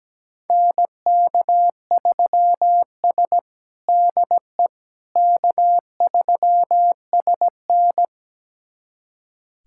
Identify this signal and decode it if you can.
Morse code, heard as a keyed tone, about 17 wpm, 700 Hz. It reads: NK3S DE K3SN
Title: USA Callsign DE Callsign - 4 Characters - 17wpm